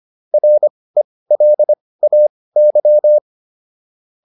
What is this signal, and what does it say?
Morse code, heard as a keyed tone, about 25 wpm, spaced as for 18 wpm, 600 Hz. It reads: RELAY